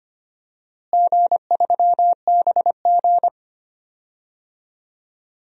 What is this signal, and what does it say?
Morse code, heard as a keyed tone, about 25 wpm, 700 Hz. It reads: G3BG
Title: DXCC-100 Call Signs - 4 Characters - Repeated 2x - 25wpm